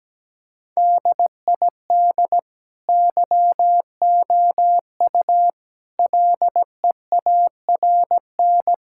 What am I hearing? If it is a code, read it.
Morse code, heard as a keyed tone, about 17 wpm, 700 Hz. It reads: DID YOU LEARN
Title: Sentences from Top 1000 Words - Review Part 5 - 17wpm